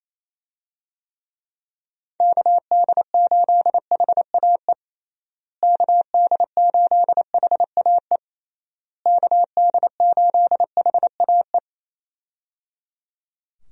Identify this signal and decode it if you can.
Morse code, heard as a keyed tone, about 27 wpm, 700 Hz. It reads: KD8HAE KD8HAE KD8HAE
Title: USA Call Signs - 6 Characters - Repeated 3x - 28wpm